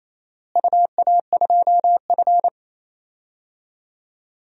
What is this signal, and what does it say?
Morse code, heard as a keyed tone, about 28 wpm, 700 Hz. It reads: UA2F
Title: DXCC-100 Call Signs - 4 Characters - Repeated 3x - 28wpm